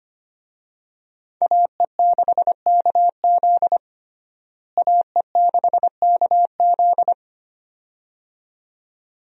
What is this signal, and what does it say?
Morse code, heard as a keyed tone, about 25 wpm, 700 Hz. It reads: AE6KZ AE6KZ
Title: USA Call Signs - 5 Characters - Repeated 2x - 25wpm